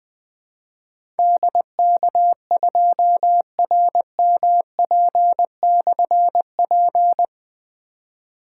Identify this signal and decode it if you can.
Morse code, heard as a keyed tone, about 20 wpm, 700 Hz. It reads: DK2RMP/P
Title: Difficult Call Signs 20wpm